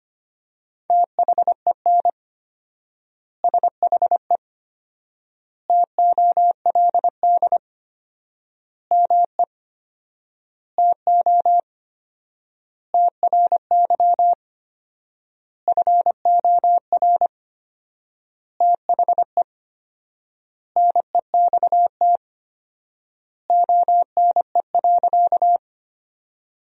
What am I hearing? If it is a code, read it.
Morse code, heard as a keyed tone, about 25 wpm, 700 Hz. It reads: THEN SHE TOLD ME TO TRY FOR THE NEXT ONE.
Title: 4x Word-Spacing - Sentences from Top 400 Words - 25wpm